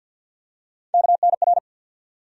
Morse code, heard as a keyed tone, about 53 wpm, 700 Hz. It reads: KNR
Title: USA Call Sign Postfixes - 3 Letters 50wpm